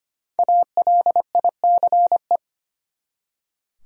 Morse code, heard as a keyed tone, about 25 wpm, 700 Hz. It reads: ALICE
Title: USA Common Names - 25wpm